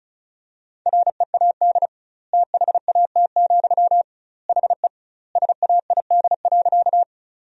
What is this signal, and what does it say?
Morse code, heard as a keyed tone, about 34 wpm, 700 Hz. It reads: READ THAT, HE SAID.
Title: 2x Word-Spacing - Sentences from Top 200 Words - 35wpm